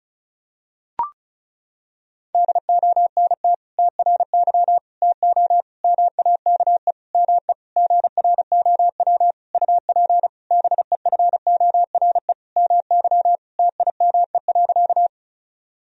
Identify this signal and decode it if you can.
Morse code, heard as a keyed tone, about 36 wpm, 700 Hz. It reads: DONT TRY TO MAKE ME GROW UP BEFORE MY TIME.
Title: Sentences from Top 1000 Words - Review Part 1 - 35wpm